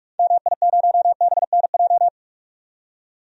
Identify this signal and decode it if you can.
Morse code, heard as a keyed tone, about 45 wpm, 700 Hz. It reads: MI0BNJ
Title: DXCC-100 Call Signs - 6 Characters - 45wpm